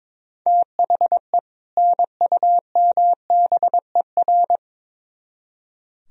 Morse code, heard as a keyed tone, about 24 wpm, 700 Hz. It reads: THE NUMBER
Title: Speed-Racing - Sets of 2 Words 22wpm